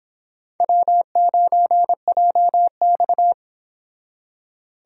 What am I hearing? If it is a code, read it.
Morse code, heard as a keyed tone, about 30 wpm, 700 Hz. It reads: W9JX